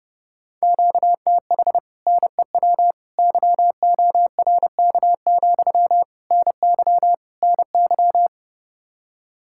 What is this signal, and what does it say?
Morse code, heard as a keyed tone, about 33 wpm, 700 Hz. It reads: QTH NEW YORK, NY NY